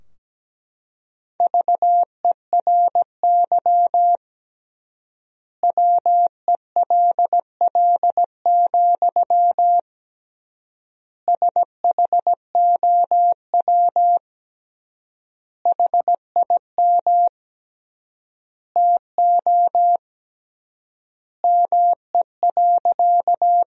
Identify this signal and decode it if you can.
Morse code, heard as a keyed tone, about 17 wpm, 700 Hz. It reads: VERY WELL, SHOW HIM TO ME.
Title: 3x Word-Spacing - Sentences from Top 200 Words - 17wpm